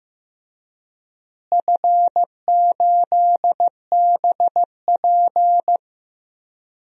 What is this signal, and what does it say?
Morse code, heard as a keyed tone, about 15 wpm, 700 Hz. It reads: F8BP
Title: DXCC-100 Call Signs - 4 Characters - Repeated 2x - 15wpm